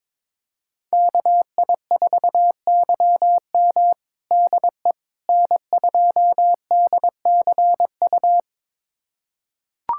Morse code, heard as a keyed tone, about 22 wpm, 700 Hz. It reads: KI4YM DE N2DCU
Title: USA Callsign DE Callsign - 5 Characters - 22wpm